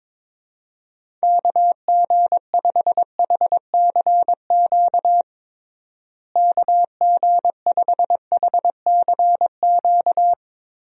Morse code, heard as a keyed tone, about 22 wpm, 700 Hz. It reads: KG5HCQ KG5HCQ
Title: USA Call Signs - 6 Characters - Repeated 2x - 22wpm